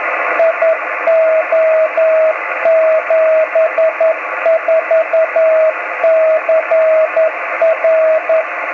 Morse code, heard as a keyed tone, about 11 wpm, 650 Hz, with radio noise. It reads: IO74CR
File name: om3pv 70mhz / zaznam / gb3cfg